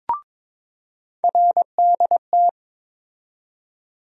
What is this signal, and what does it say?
Morse code, heard as a keyed tone, about 23 wpm, 700 Hz. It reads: RDT
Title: USA Call Sign Postfixes - 3 Letters 22wpm